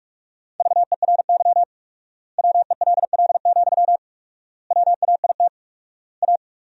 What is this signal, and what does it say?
Morse code, heard as a keyed tone, about 43 wpm, 700 Hz. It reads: VERY WELL, WAIT A
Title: 4x Word-Spacing - Sentences from Top 500 Words - 45wpm